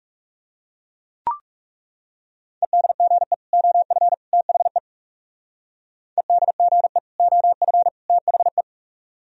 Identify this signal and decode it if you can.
Morse code, heard as a keyed tone, about 39 wpm, 700 Hz. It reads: EDGE OF THE EDGE OF THE